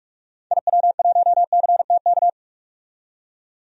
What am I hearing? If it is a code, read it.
Morse code, heard as a keyed tone, about 43 wpm, 700 Hz. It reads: IW1CTK
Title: DXCC-100 Call Signs - 6 Characters - 45wpm